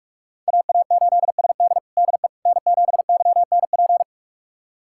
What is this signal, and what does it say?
Morse code, heard as a keyed tone, about 42 wpm, 700 Hz. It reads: AA8SD DE N7YNP
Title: USA Callsign DE Callsign - 5 Characters - 45wpm